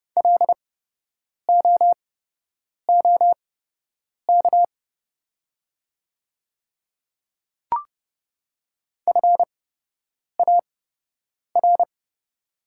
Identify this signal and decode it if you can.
Morse code, heard as a keyed tone, about 30 wpm, spaced as for 8 wpm, 700 Hz. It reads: LOOK FAR
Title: 8x Character Spacing - Top 300 Words - 30wpm